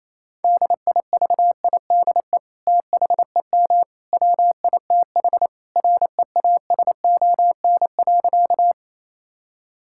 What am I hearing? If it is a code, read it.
Morse code, heard as a keyed tone, about 28 wpm, 700 Hz. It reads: DIVIDE THEM WITH REASON.
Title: Sentences from Top 1600 Words 28wpm